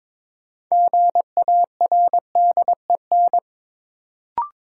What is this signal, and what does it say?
Morse code, heard as a keyed tone, about 22 wpm, 700 Hz. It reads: GARDEN
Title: Sentences from Top 700 Words - New Word Drill - 22wpm